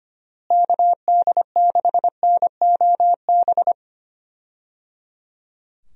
Morse code, heard as a keyed tone, about 25 wpm, 700 Hz. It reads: KD6NOB
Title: USA Call Signs - 6 Characters - Repeated 3x - 25wpm